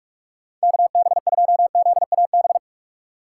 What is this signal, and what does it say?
Morse code, heard as a keyed tone, about 46 wpm, 700 Hz. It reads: KB2ZAB
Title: USA Call Signs - 6 Characters - 3x Character Spacing - 45wpm